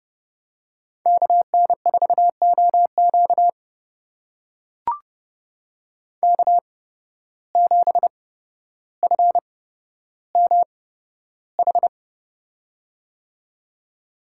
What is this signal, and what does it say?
Morse code, heard as a keyed tone, about 31 wpm, spaced as for 13 wpm, 700 Hz. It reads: KN4OQ K7FMH
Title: USA Call Signs - 5 Characters - 8x Character Spacing - 30wpm